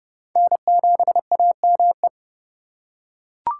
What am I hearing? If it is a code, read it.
Morse code, heard as a keyed tone, about 30 wpm, 700 Hz. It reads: N7AME
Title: K1USN SST QSO Practice - 30wpm